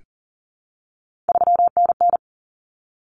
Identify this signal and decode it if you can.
Morse code, heard as a keyed tone, about 40 wpm, 700 Hz. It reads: 3NN
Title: Common Cut Numbers - 3-Digit 40wpm